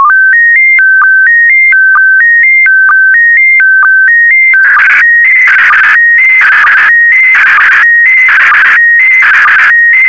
SSTV (slow-scan television), PD 290.
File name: sstv / PD290-VK4MSL